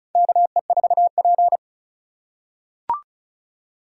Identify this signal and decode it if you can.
Morse code, heard as a keyed tone, about 34 wpm, 700 Hz. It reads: KE4P
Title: USA Call Signs - 4 Characters - 4x Character Spacing - 35wpm